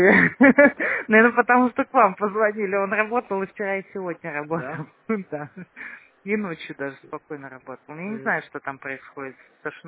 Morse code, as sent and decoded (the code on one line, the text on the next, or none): none